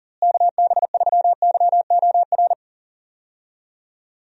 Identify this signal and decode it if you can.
Morse code, heard as a keyed tone, about 38 wpm, 700 Hz. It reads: KB3YOR
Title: USA Call Signs - 6 Characters - Repeated 2x - 40wpm